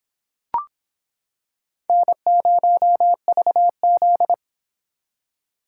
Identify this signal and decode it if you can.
Morse code, heard as a keyed tone, about 25 wpm, 700 Hz. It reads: N0VZ